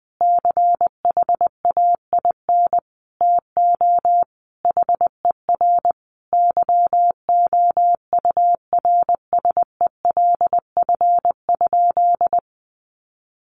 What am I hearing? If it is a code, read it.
Morse code, heard as a keyed tone, about 20 wpm, 700 Hz. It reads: CHAIN TO HER YOURSELF?
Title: Sentences from Top 1500 Words 20wpm